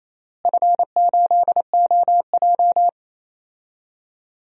Morse code, heard as a keyed tone, about 28 wpm, 700 Hz. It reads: F8OJ
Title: DXCC-100 Call Signs - 4 Characters - Repeated 3x - 28wpm